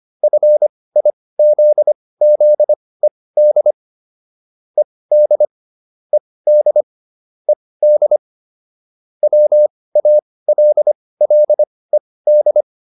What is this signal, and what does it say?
Morse code, heard as a keyed tone, about 25 wpm, spaced as for 18 wpm, 600 Hz. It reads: FIZZED ED ED ED WALLED